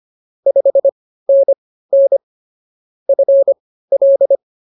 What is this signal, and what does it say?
Morse code, heard as a keyed tone, about 24 wpm, spaced as for 15 wpm, 550 Hz. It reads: 5NN FL